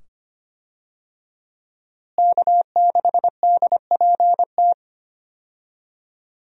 Morse code, heard as a keyed tone, about 25 wpm, 700 Hz. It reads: K6DPT
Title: USA Call Signs - 5 Characters - Repeated 2x - 25wpm